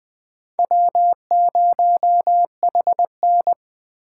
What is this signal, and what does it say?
Morse code, heard as a keyed tone, about 25 wpm, 700 Hz. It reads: W0HN
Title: Speed Racing - USA 1x2 Call Signs 20wpm